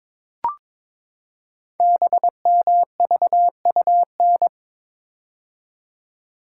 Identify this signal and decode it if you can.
Morse code, heard as a keyed tone, about 22 wpm, 700 Hz. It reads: BMVUN
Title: Code Groups - 5 Letters - 22wpm